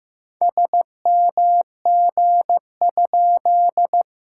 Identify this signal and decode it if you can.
Morse code, heard as a keyed tone, about 15 wpm, 700 Hz. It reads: SMG?
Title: SOTA QSO Practice 15wpm